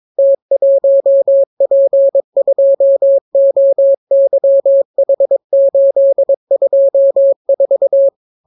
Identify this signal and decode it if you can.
Morse code, heard as a keyed tone, about 22 wpm, 550 Hz. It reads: T1P2OYH824